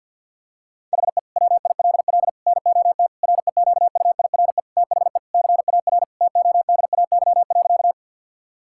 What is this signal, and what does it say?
Morse code, heard as a keyed tone, about 52 wpm, 700 Hz. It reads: HE WILL NOT REQUIRE THE CAR TODAY.
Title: Sentences from Top 1000 Words - Review Part 6 - 50wpm